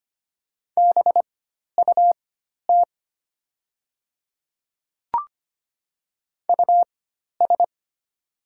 Morse code, heard as a keyed tone, about 25 wpm, spaced as for 12 wpm, 700 Hz. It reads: BUT US
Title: 4x Character Spacing - Top 200 Words - 25wpm